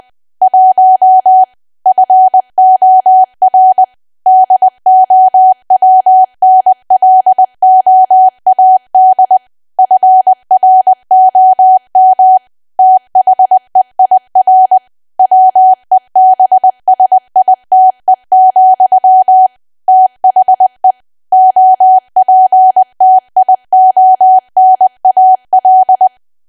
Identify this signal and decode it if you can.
Morse code, heard as a keyed tone, about 20 wpm, 750 Hz. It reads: 1 FOR DOWNLOAD FROM THEIR WEBSITE, THE OPTIONAL
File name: Archive / 20 WPM / 20WPM